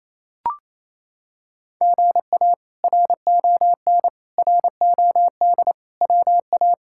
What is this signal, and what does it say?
Morse code, heard as a keyed tone, about 28 wpm, 700 Hz. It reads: GA RON ROD WA
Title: K1USN SST QSO Practice - 28wpm